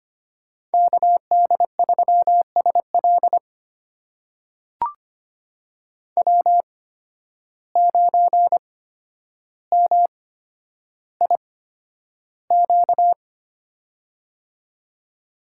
Morse code, heard as a keyed tone, about 25 wpm, spaced as for 11 wpm, 700 Hz. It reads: KD3SL W9MIQ
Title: USA Call Signs - 5 Characters - 8x Character Spacing - 25wpm